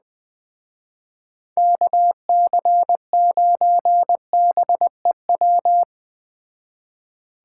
Morse code, heard as a keyed tone, about 21 wpm, 700 Hz. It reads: KC9BEW